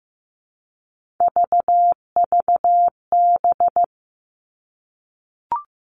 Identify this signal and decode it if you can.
Morse code, heard as a keyed tone, about 15 wpm, 700 Hz. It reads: VVB
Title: Three Challenging Letter-Number Combinations 15wpm